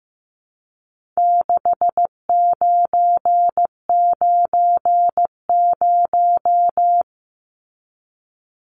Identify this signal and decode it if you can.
Morse code, heard as a keyed tone, about 15 wpm, 700 Hz. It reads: 6990
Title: Numbers - 4-Digit 15wpm